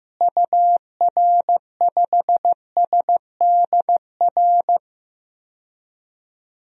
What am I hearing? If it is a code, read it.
Morse code, heard as a keyed tone, about 16 wpm, 700 Hz. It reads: UR5SDR
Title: DXCC-100 Call Signs - 6 Characters - Repeated 3x - 15wpm